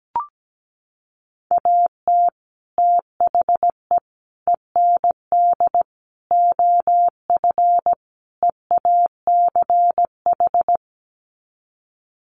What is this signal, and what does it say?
Morse code, heard as a keyed tone, about 17 wpm, 700 Hz. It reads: AT THE END OF EACH
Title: Sets of 5 Words Encore 17wpm